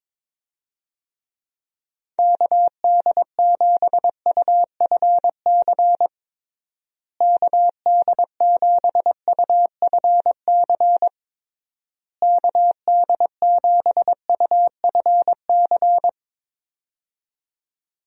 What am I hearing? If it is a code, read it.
Morse code, heard as a keyed tone, about 22 wpm, 700 Hz. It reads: KD7UFC KD7UFC KD7UFC
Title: USA Call Signs - 6 Characters - Repeated 3x - 22wpm